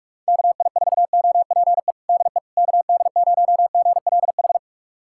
Morse code, heard as a keyed tone, about 42 wpm, 700 Hz. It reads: KI4OPE DE KD0GLH
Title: USA Callsign DE Callsign 45wpm